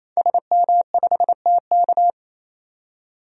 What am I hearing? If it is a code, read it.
Morse code, heard as a keyed tone, about 28 wpm, 700 Hz. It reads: SM5TK
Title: DXCC-100 Call Signs - 5 Characters - 28wpm